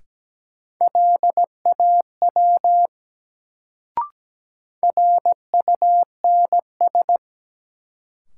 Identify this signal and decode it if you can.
Morse code, heard as a keyed tone, about 17 wpm, 700 Hz. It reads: LAW RUNS
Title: Sentences from Top 600 Words - New Word Drill - 17wpm